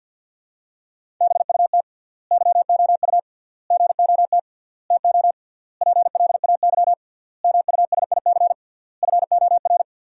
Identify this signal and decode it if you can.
Morse code, heard as a keyed tone, about 51 wpm, 700 Hz. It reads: BUT YOU GOT TO PLAY MUSIC FOR
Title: 3x Word-Spacing - Sentences from Top 300 Words - 50wpm